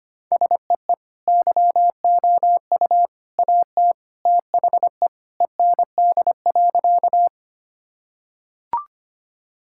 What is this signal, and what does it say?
Morse code, heard as a keyed tone, about 25 wpm, 700 Hz. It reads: SEE YOU AT THE END.
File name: practice / 25wpm / Sentences with 3-Word Phrases - 25wpm